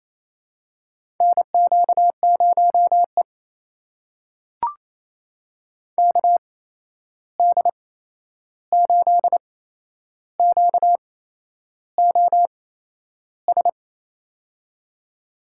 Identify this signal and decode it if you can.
Morse code, heard as a keyed tone, about 28 wpm, spaced as for 11 wpm, 700 Hz. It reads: NQ0E KD8QOS
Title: USA Call Signs - 8x Character Spacing - 28wpm